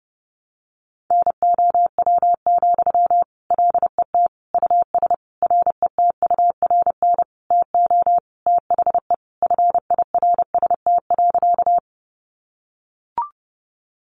Morse code, heard as a keyed tone, about 28 wpm, 700 Hz. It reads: NOW, LET US RETURN TO THE FIRST.
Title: Sentences from Top 800 Words 30wpm